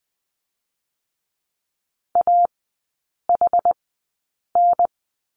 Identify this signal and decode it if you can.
Morse code, heard as a keyed tone, about 20 wpm, spaced as for 8 wpm, 700 Hz. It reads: AHN
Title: 133 Lesson 22e - Question Mark - Three Characters 20wpm